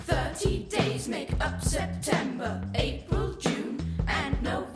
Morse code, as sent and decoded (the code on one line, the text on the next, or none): none